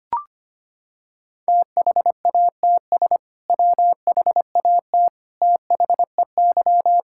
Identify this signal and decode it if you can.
Morse code, heard as a keyed tone, about 25 wpm, 700 Hz. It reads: THATS WHAT THEY
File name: practice / 25wpm / Sentences with 2-Word Phrases - 25wpm